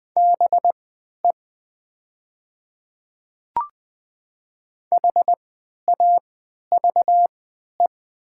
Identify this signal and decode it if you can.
Morse code, heard as a keyed tone, about 20 wpm, spaced as for 11 wpm, 700 Hz. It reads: BE HAVE
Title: 3x Character Spacing - Top 100 Words - 20wpm